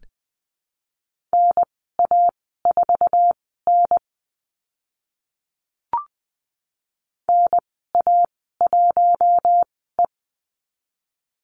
Morse code, heard as a keyed tone, about 20 wpm, spaced as for 14 wpm, 700 Hz. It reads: NA4N NA1E
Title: 055 Lesson 10g - Letter r - Call Signs 20wpm